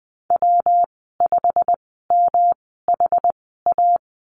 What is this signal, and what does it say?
Morse code, heard as a keyed tone, about 20 wpm, spaced as for 15 wpm, 700 Hz. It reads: W5MHA
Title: 122 Lesson 20g - Number 3 - Call Signs 20wpm